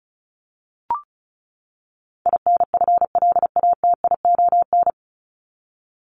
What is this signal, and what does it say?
Morse code, heard as a keyed tone, about 34 wpm, 700 Hz. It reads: INFLATION
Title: Top 1001-1250 Words 35wpm